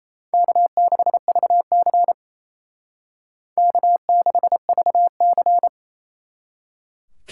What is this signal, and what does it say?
Morse code, heard as a keyed tone, about 35 wpm, 700 Hz. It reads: K6VC K6VC
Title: Speed Racing - USA 1x2 Call Signs 28wpm